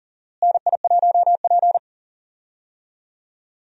Morse code, heard as a keyed tone, about 40 wpm, 700 Hz. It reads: NI1P
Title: USA Call Signs - 4 Characters - Repeated 3x - 40wpm